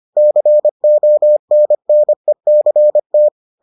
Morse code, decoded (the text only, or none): CONNECT